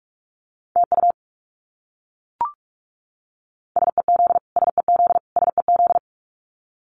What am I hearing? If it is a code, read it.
Morse code, heard as a keyed tone, about 44 wpm, 700 Hz. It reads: TU SEZ SEZ SEZ
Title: Abbreviations Repeated 3x - 45wpm